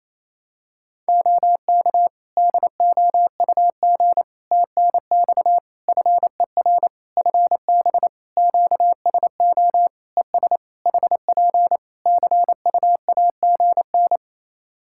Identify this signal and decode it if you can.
Morse code, heard as a keyed tone, about 28 wpm, 700 Hz. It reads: OK DOUG TNX FER FB QSO ES HP CUAGN
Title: K1USN SST QSO Practice 28wpm